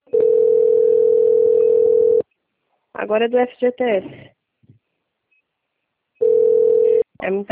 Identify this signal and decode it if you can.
Ringback tone (North America), 440 + 480 Hz, with speech.
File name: out- - - - - .4188